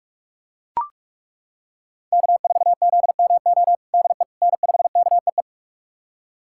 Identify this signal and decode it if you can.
Morse code, heard as a keyed tone, about 41 wpm, 700 Hz. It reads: K4ZMO DE N5KEE
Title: USA Callsign DE Callsign - 5 Characters - 45wpm